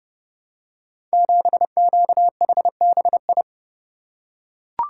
Morse code, heard as a keyed tone, about 29 wpm, 700 Hz. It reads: 7QHBI
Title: Code Groups - 5 Letter:Numbers - 30wpm